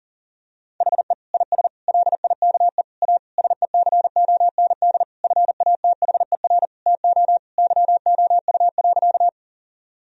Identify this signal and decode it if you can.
Morse code, heard as a keyed tone, about 40 wpm, 700 Hz. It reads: HE IS LIKE A SECOND FATHER TO YOU.
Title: Sentences from Top 1000 Words - Review Part 1 - 40wpm